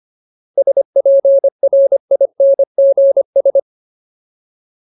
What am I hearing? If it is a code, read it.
Morse code, heard as a keyed tone, about 25 wpm, 550 Hz. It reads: SPRINGS
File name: Ultimate Word List / 25WPM Ultimate